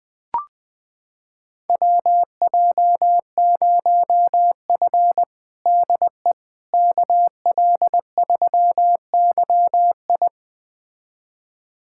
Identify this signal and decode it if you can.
Morse code, heard as a keyed tone, about 20 wpm, 700 Hz. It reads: WJ0F DE KL3YI
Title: USA Callsign DE Callsign 20wpm